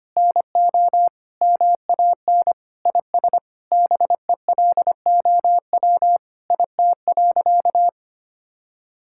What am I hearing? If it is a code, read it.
Morse code, heard as a keyed tone, about 25 wpm, 700 Hz. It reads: NO MAN IS BELOW IT.
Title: Sentences from Top 1000 Words - Review Part 5 - 25wpm